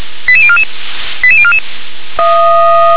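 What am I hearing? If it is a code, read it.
Five-tone selective call (ZVEI), heard as quick runs of five tones, tones of about 70 ms. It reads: 88044 88044
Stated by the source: ZVEI five-tone selective call received on SDR.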